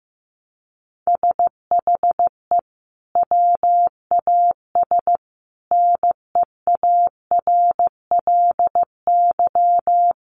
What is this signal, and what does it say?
Morse code, heard as a keyed tone, about 15 wpm, 700 Hz. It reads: SHE WAS NEARLY
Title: Sentences from Top 1300 Words 15wpm